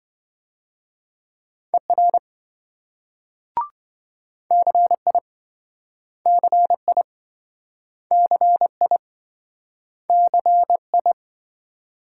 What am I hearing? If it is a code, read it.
Morse code, heard as a keyed tone, about 27 wpm, 700 Hz. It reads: ER CI CI CI CI